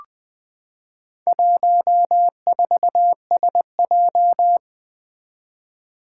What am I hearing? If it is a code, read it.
Morse code, heard as a keyed tone, about 20 wpm, 700 Hz. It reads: 14SJ
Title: Four Challenging Letter-Number Combinations 20wpm